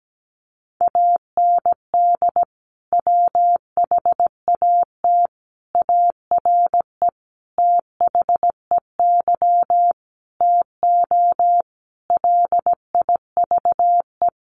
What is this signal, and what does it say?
Morse code, heard as a keyed tone, about 17 wpm, 700 Hz. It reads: AND WHAT ARE THEY TO LIVE
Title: Sentences from Top 1100 Words 17wpm